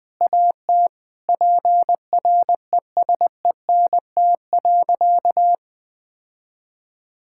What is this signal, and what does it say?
Morse code, heard as a keyed tone, about 19 wpm, 700 Hz. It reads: AT PRESENT.
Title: Sentences from Top 1000 Words - Review Part 5 - 20wpm